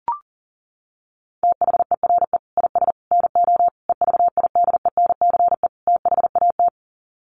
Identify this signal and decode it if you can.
Morse code, heard as a keyed tone, about 40 wpm, 700 Hz. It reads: THERE IS NO EVIDENCE THAT
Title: Sets of 5 Words Encore 40wpm